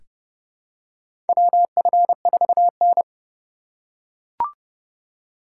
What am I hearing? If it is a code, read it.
Morse code, heard as a keyed tone, about 31 wpm, 700 Hz. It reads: WF4N
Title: USA Call Signs - 4 Characters - 2x Character Spacing - 30wpm